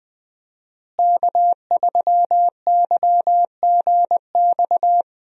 Morse code, heard as a keyed tone, about 20 wpm, 700 Hz. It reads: K3YGX
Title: USA Call Signs - 5 Characters - Repeated 2x - 20wpm